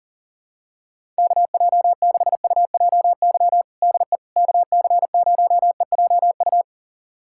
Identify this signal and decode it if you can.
Morse code, heard as a keyed tone, about 40 wpm, 700 Hz. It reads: KJ6UJY DE KC0EJU
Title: USA Callsign DE Callsign - 6 Characters - 40wpm